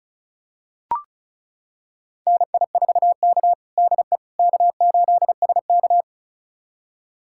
Morse code, heard as a keyed tone, about 36 wpm, 700 Hz. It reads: NI4K DE K8SK
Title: USA Callsign DE Callsign - 4 Characters - 35wpm